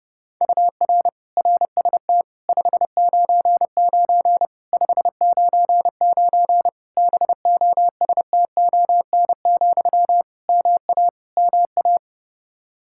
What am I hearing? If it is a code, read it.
Morse code, heard as a keyed tone, about 30 wpm, 700 Hz. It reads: UR RST 599 599 BOSTON, MA MA
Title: Practice QSO - Exchange 1 - Beginning - 30wpm